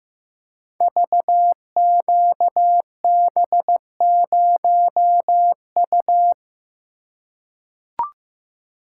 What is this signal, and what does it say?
Morse code, heard as a keyed tone, about 15 wpm, 700 Hz. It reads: VQB0U
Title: Code Groups - 5 Letter:Numbers - 15wpm